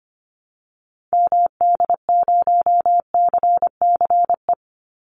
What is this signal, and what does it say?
Morse code, heard as a keyed tone, about 25 wpm, 700 Hz. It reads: MD0CCE
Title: Call Signs Repeated 2x - 25wpm